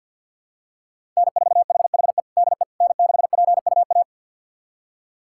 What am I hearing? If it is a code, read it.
Morse code, heard as a keyed tone, about 51 wpm, 700 Hz. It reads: N4HHE DE N6PUA
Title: USA Callsign DE Callsign - 5 Characters - 50wpm